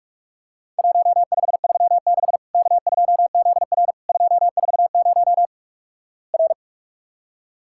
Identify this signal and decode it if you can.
Morse code, heard as a keyed tone, about 46 wpm, 700 Hz. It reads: 1536 K2ZR 240 R
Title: Worked All Europe DX Contest - QTC Exchange - 45wpm